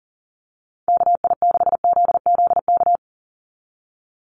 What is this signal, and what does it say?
Morse code, heard as a keyed tone, about 40 wpm, 700 Hz. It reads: KI6ZZK
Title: USA Call Signs 40wpm